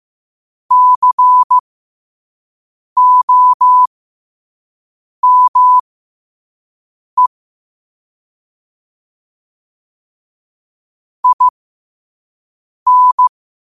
Morse code, heard as a keyed tone, about 15 wpm, spaced as for 5 wpm, 1000 Hz. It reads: COME IN